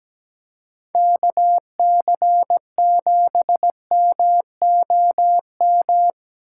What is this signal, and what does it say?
Morse code, heard as a keyed tone, about 20 wpm, 700 Hz. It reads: KC7MOM
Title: Speed Racing - USA 2x3 Call Signs 17wpm